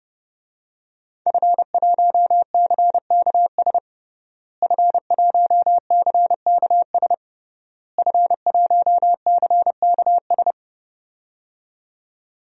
Morse code, heard as a keyed tone, about 30 wpm, 700 Hz. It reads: F1CKS F1CKS F1CKS
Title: DXCC-100 Call Signs - 5 Characters - Repeated 3x - 30wpm